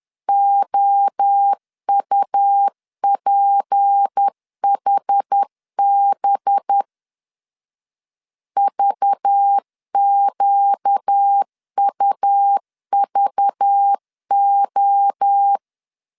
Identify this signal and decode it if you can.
Morse code, heard as a keyed tone, about 10 wpm, 800 Hz. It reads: OUPHB VQUVO
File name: CW / DET / 1 6 1500C S B 10wpm 4137